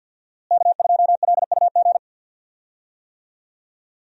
Morse code, heard as a keyed tone, about 48 wpm, 700 Hz. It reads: K2LUG
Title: USA Call Signs - 5 Characters - Repeated 3x - 50wpm